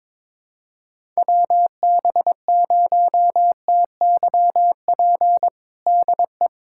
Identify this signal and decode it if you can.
Morse code, heard as a keyed tone, about 22 wpm, 700 Hz. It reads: WB0TYP DE
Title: USA Callsign DE Callsign - 6 Characters - 22wpm